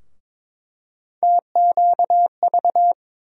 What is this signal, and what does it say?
Morse code, heard as a keyed tone, about 22 wpm, 700 Hz. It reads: TQV
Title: USA Call Sign Postfixes - 3 Letters 22wpm